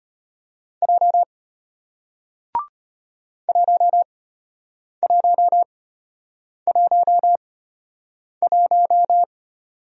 Morse code, heard as a keyed tone, about 34 wpm, spaced as for 8 wpm, 700 Hz. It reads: J 1111